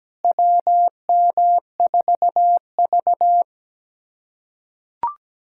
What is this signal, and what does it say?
Morse code, heard as a keyed tone, about 17 wpm, 700 Hz. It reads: WM4V
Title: USA Callsign DE Callsign - 4 Characters - 17wpm